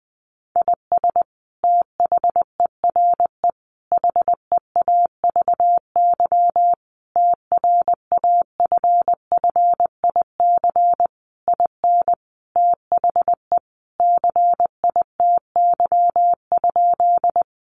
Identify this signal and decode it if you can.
Morse code, heard as a keyed tone, about 20 wpm, 700 Hz. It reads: IS THERE HEAVY TRAFFIC IN THE CITY?
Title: Sentences from Top 1300 Words 20wpm